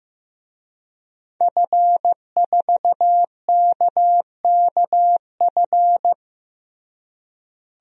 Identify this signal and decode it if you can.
Morse code, heard as a keyed tone, about 16 wpm, 700 Hz. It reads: F4KKF
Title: DXCC-100 Call Signs - 5 Characters - Repeated 2x - 15wpm